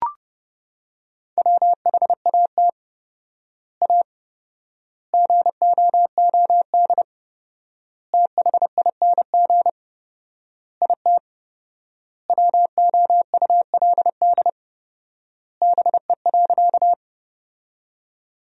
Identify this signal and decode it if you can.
Morse code, heard as a keyed tone, about 30 wpm, 700 Hz. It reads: WHAT A GOOD THING IT WOULD BE.
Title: 4x Word-Spacing - Sentences from Top 200 Words - 30wpm